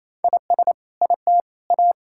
Morse code, heard as a keyed tone, about 28 wpm, 700 Hz. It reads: IS IT A
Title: Sentences from Top 1000 Words - Review Part 6 - 28wpm